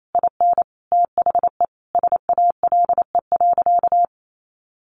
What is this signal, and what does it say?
Morse code, heard as a keyed tone, about 28 wpm, 700 Hz. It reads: IN THE SALE.
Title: Sentences from Top 1400 Words 28wpm